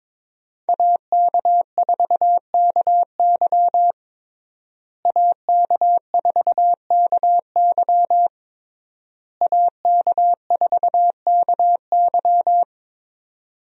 Morse code, heard as a keyed tone, about 22 wpm, 700 Hz. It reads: AK4KY AK4KY AK4KY
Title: USA Call Signs - 5 Characters - Repeated 3x - 22wpm